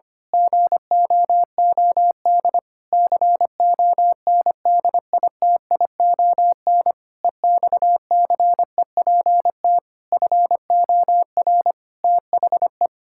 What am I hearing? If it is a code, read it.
Morse code, heard as a keyed tone, about 25 wpm, 700 Hz. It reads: GOOD CONDITION EXCEPT FOR THE
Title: Pangram Challenge v2 - 25wpm